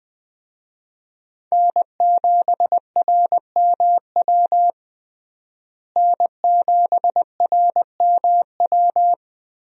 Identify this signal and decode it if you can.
Morse code, heard as a keyed tone, about 20 wpm, 700 Hz. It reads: N7RMW N7RMW
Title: USA Call Signs - 5 Characters - Repeated 3x - 20wpm